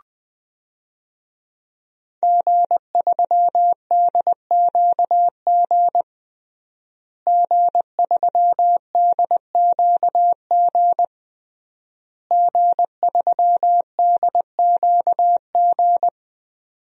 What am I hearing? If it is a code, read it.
Morse code, heard as a keyed tone, about 20 wpm, 700 Hz. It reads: G3DQG G3DQG G3DQG
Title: DXCC-100 Call Signs - 5 Characters - Repeated 3x - 20wpm